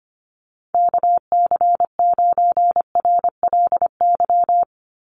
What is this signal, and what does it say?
Morse code, heard as a keyed tone, about 25 wpm, 700 Hz. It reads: KC9RLY
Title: USA Call Signs 25wpm